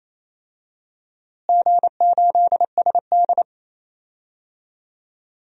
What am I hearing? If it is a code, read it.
Morse code, heard as a keyed tone, about 28 wpm, 700 Hz. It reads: G8SD
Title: DXCC-100 Call Signs - 4 Characters - Repeated 2x - 28wpm